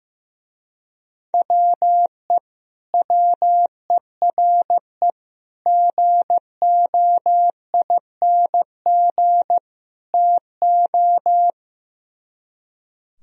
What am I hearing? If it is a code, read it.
Morse code, heard as a keyed tone, about 15 wpm, 700 Hz. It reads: WE WERE GOING TO
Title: Speed-Racing - Sets of 4 Words 15wpm